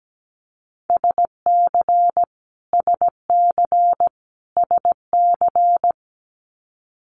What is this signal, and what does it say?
Morse code, heard as a keyed tone, about 17 wpm, 700 Hz. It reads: SC SC SC
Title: Multi-letter Phonemes Repeated 3x 17wpm